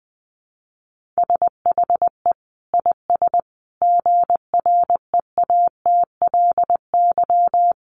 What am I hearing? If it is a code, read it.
Morse code, heard as a keyed tone, about 20 wpm, 700 Hz. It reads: SHE IS GREATLY
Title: Sentences from Top 1400 Words 20wpm